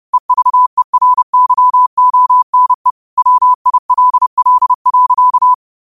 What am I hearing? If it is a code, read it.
Morse code, heard as a keyed tone, about 30 wpm, 1000 Hz. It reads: EVERYONE WILL.
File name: qotdfiles / 30wpm